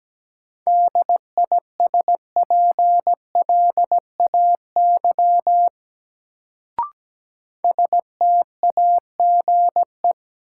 Morse code, heard as a keyed tone, about 17 wpm, 700 Hz. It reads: DISPLAY STAGE
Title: Sentences from Top 1200 Words - New Word Drill Alt - 17wpm